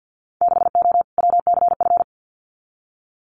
Morse code, heard as a keyed tone, about 50 wpm, 700 Hz. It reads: 6O WCF
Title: Winter Field Day Exchanges Encore 50wpm